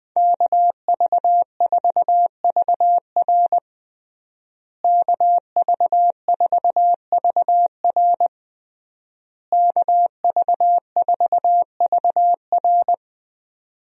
Morse code, heard as a keyed tone, about 21 wpm, 700 Hz. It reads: KV4VR KV4VR KV4VR
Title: USA Call Signs - 5 Characters - Repeated 3x - 20wpm